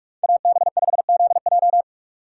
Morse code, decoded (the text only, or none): AB5ZJ